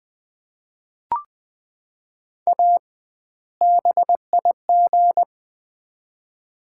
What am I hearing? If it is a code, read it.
Morse code, heard as a keyed tone, about 20 wpm, 700 Hz. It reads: A BIG
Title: 2x Word-Spacing - Sets of 2 Words - 20wpm